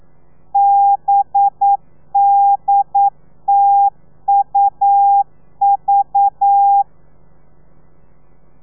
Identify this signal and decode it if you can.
Morse code, heard as a keyed tone, about 9 wpm, 800 Hz. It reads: BDTUV
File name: telegrafia / i2tfj / Lez4 40